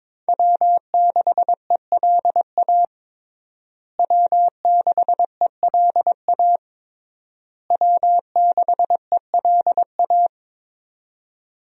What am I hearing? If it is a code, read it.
Morse code, heard as a keyed tone, about 22 wpm, 700 Hz. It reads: W6ELA W6ELA W6ELA
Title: USA Call Signs - 5 Characters - Repeated 3x - 22wpm